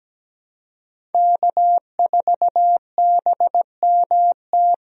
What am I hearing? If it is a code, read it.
Morse code, heard as a keyed tone, about 17 wpm, 700 Hz. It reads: K4BMT
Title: USA Call Signs - 5 Characters - Repeated 3x - 17wpm